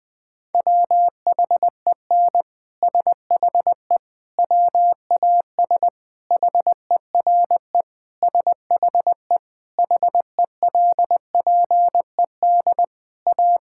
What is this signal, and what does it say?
Morse code, heard as a keyed tone, about 20 wpm, 700 Hz. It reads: WHEN SHE WAS HERE SHE HELPED A
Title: Sentences with 3-Word Phrases - Spoken Beginning - 20wpm